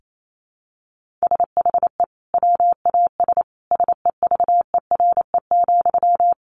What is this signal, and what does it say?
Morse code, heard as a keyed tone, about 28 wpm, 700 Hz. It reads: SHE WAS SEVERE,
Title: Sentences from Top 1400 Words 28wpm